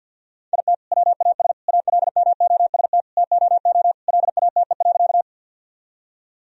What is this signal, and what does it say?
Morse code, heard as a keyed tone, about 50 wpm, 700 Hz. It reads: IT WAS ALMOST TOO LATE.
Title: Sentences from Top 1000 Words - Review Part 4 - 50wpm